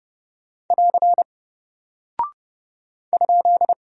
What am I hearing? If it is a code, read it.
Morse code, heard as a keyed tone, about 30 wpm, 700 Hz. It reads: <AR> ?
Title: Single Character (Letter:Number:Punctuation:Prosign) Alt 30wpm